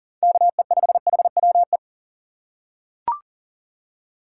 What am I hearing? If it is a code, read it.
Morse code, heard as a keyed tone, about 40 wpm, 700 Hz. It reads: KE5HWE
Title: USA Call Signs - 6 Characters - 2x Character Spacing - 40wpm